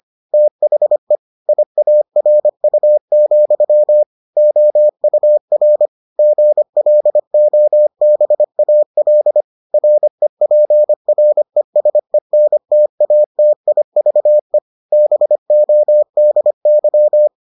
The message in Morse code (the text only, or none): THE IARU, OUR GLOBAL REPRESENTATIVE BODY